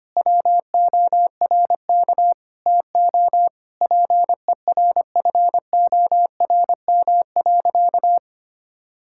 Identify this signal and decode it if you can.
Morse code, heard as a keyed tone, about 25 wpm, 700 Hz. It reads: WORK TO PERFORM.
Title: Sentences from Top 1000 Words - Review Part 5 - 25wpm